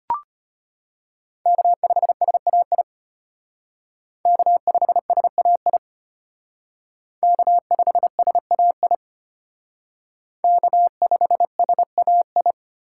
Morse code, heard as a keyed tone, about 33 wpm, 700 Hz. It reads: K5SAI K5SAI K5SAI K5SAI